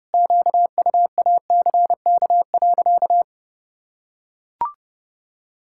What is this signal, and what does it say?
Morse code, heard as a keyed tone, about 30 wpm, 700 Hz. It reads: QUACK.
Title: Pangram Challenge v2 - 30wpm